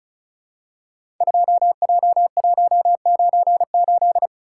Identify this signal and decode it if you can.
Morse code, heard as a keyed tone, about 36 wpm, 700 Hz. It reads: 2J198
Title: Code Groups - 5 Letter:Numbers - 35wpm